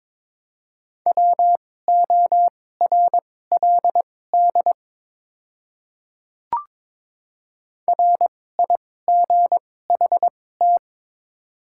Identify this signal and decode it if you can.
Morse code, heard as a keyed tone, about 22 wpm, spaced as for 16 wpm, 700 Hz. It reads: WORLD RIGHT
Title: 2x Character Spacing - Top 200 Words - 22wpm